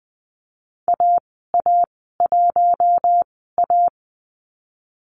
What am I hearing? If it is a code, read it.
Morse code, heard as a keyed tone, about 20 wpm, spaced as for 15 wpm, 700 Hz. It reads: AA1A
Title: 055 Lesson 10g - Letter r - Call Signs 20wpm